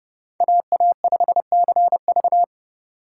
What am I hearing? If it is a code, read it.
Morse code, heard as a keyed tone, about 28 wpm, 700 Hz. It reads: AA5CV